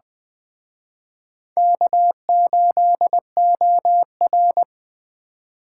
Morse code, heard as a keyed tone, about 20 wpm, 700 Hz. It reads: K8OR
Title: Speed Racing - USA 1x2 Call Signs 17wpm